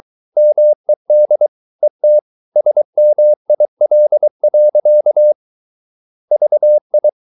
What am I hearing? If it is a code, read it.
Morse code, heard as a keyed tone, about 23 wpm, 600 Hz. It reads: MED ET SMIL. VI
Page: Ferie 23 wpm